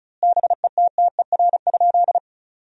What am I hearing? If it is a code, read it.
Morse code, heard as a keyed tone, about 36 wpm, 700 Hz. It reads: BETTER?
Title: 2x Word-Spacing - Sentences from Top 400 Words - 35wpm